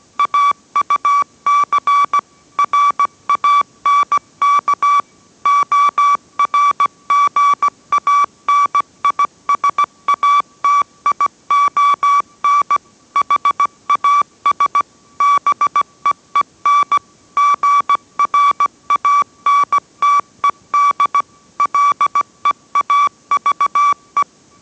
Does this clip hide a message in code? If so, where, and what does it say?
Yes: Morse code from 0.2 to 24.2 s, heard as a keyed tone, about 17 wpm, 1150 Hz. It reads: AUC RANK ORGANISATION HAS BEEN GRANTED LEAVE